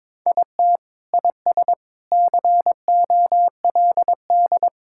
Morse code, heard as a keyed tone, about 21 wpm, 700 Hz. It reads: IT IS COLD
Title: Sentences from Top 1000 Words - Review Part 6 - 22wpm